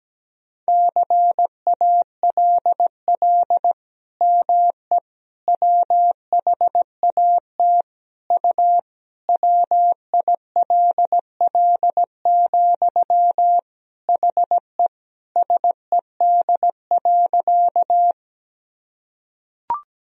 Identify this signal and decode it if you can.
Morse code, heard as a keyed tone, about 17 wpm, 700 Hz. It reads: CALL ME WHAT U WILL, HE SED.
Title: Sentences from Top 200 Words and Abbreviations 17wpm